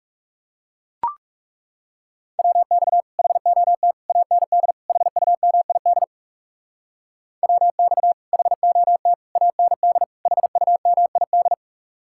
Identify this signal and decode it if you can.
Morse code, heard as a keyed tone, about 40 wpm, 700 Hz. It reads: WX HOT AND HUMID WX HOT AND HUMID